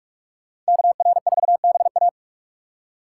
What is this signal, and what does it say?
Morse code, heard as a keyed tone, about 56 wpm, 700 Hz. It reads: KR4BA
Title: Speed Racing - USA 2x2 Call Signs 45wpm